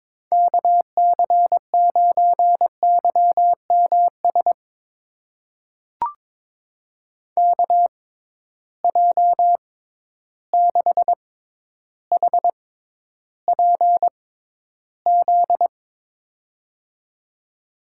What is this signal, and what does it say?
Morse code, heard as a keyed tone, about 22 wpm, spaced as for 11 wpm, 700 Hz. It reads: KC9YMS KJ6HPZ
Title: USA Call Signs - 6 Characters - 6x Character Spacing - 22wpm